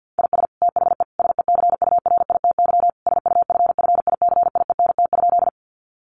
Morse code, heard as a keyed tone, about 50 wpm, 700 Hz. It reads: IS THE SECURITY SUFFICIENT?
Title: Sentences from Top 1400 Words 50wpm